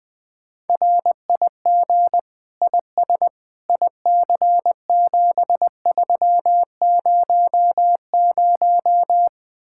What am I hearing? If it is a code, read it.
Morse code, heard as a keyed tone, about 21 wpm, 700 Hz. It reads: RIG IS IC7300
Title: Speed-Racing Common QSO Phrases Encore - 20wpm